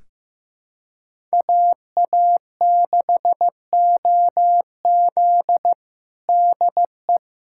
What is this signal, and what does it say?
Morse code, heard as a keyed tone, about 15 wpm, 700 Hz. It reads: AA6OZ DE
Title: USA Callsign DE Callsign 15wpm